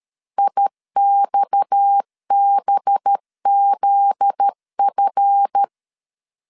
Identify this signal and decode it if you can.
Morse code, heard as a keyed tone, about 13 wpm, 800 Hz. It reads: IXBZF
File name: LEZIONI / 9 2 12wpm 2301